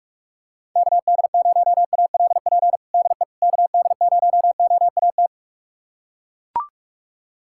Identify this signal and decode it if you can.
Morse code, heard as a keyed tone, about 41 wpm, 700 Hz. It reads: KD0ALP DE KD0OAT
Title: USA Callsign DE Callsign 45wpm